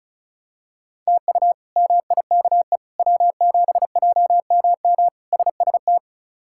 Morse code, heard as a keyed tone, about 34 wpm, 700 Hz. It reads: TU MIKE W7JMM SST
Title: K1USN SST QSO Practice - 35wpm